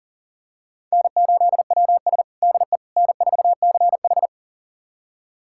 Morse code, heard as a keyed tone, about 40 wpm, 700 Hz. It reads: N8WS DE N4CH
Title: USA Callsign DE Callsign - 4 Characters - 40wpm